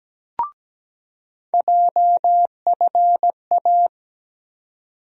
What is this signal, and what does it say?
Morse code, heard as a keyed tone, about 17 wpm, 700 Hz. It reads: JFA
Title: USA Call Sign Postfixes - 3 Letters 17wpm